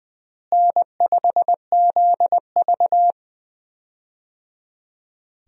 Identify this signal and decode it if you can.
Morse code, heard as a keyed tone, about 20 wpm, 700 Hz. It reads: N5ZV
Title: USA Call Signs - 4 Characters - Repeated 3x - 20wpm